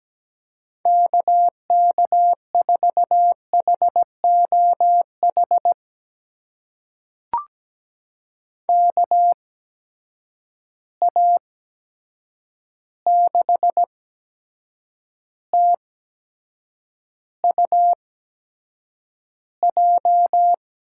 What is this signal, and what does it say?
Morse code, heard as a keyed tone, about 17 wpm, spaced as for 7 wpm, 700 Hz. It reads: KK4HOH KA6TUJ
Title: USA Call Signs - 6 Characters - 8x Character Spacing - 17wpm